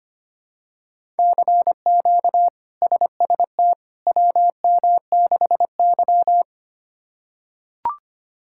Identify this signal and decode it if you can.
Morse code, heard as a keyed tone, about 25 wpm, 700 Hz. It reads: CQ SST WM6Y
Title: K1USN SST QSO Practice - 25wpm